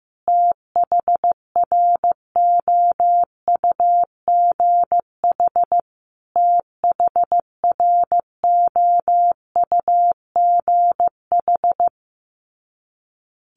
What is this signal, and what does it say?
Morse code, heard as a keyed tone, about 15 wpm, 700 Hz. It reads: THROUGH THROUGH
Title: Top 500 Words Repeated 15wpm